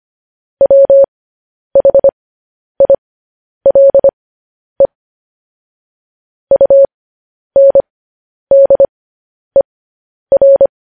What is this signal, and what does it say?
Morse code, heard as a keyed tone, about 25 wpm, spaced as for 10 wpm, 550 Hz. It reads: WHILE UNDER
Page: news-headlines-in-morse-code-at-10-wpm